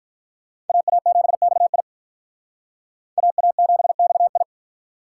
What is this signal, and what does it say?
Morse code, heard as a keyed tone, about 45 wpm, 700 Hz. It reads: AA7XI AA7XI